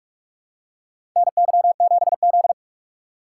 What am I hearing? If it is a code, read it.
Morse code, heard as a keyed tone, about 42 wpm, 700 Hz. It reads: NY7Z
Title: K1USN SST QSO Practice - 45wpm